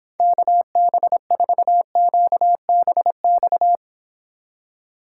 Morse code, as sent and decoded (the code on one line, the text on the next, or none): -.- -... ....- --.- -... -..-
KB4QBX